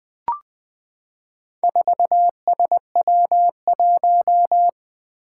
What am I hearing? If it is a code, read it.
Morse code, heard as a keyed tone, about 21 wpm, 700 Hz. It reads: 4SW1
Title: Four Challenging Letter-Number Combinations 20wpm